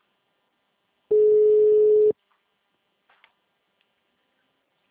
Ringback tone (Europe, 425 Hz).